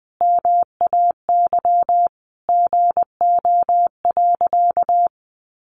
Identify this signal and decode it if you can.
Morse code, heard as a keyed tone, about 20 wpm, 700 Hz. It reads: MAY GO.
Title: Sentences from Top 100 Words 20wpm